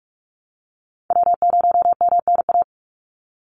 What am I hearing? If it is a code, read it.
Morse code, heard as a keyed tone, about 44 wpm, 700 Hz. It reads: W0MNA
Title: NA SOTA Chaser Callsigns 45wpm